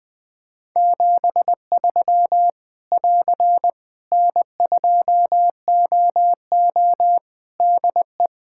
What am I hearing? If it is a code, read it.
Morse code, heard as a keyed tone, about 20 wpm, 700 Hz. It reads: 73 <AR> N2OO DE
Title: Practice QSO - Exchange 4 - Ending - 20wpm